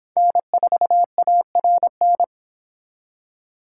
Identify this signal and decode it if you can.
Morse code, heard as a keyed tone, about 28 wpm, 700 Hz. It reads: N4ARN